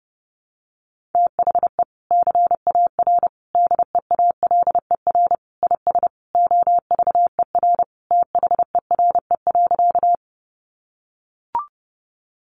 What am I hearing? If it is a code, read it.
Morse code, heard as a keyed tone, about 30 wpm, 700 Hz. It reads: THE CAR DEALER IS OVER THERE.
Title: Sentences from Top 1100 Words 30wpm